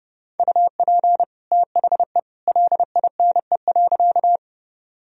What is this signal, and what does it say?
Morse code, heard as a keyed tone, about 30 wpm, 700 Hz. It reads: UP THE LINE.
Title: Sentences from Top 1000 Words - Review Part 1 - 30wpm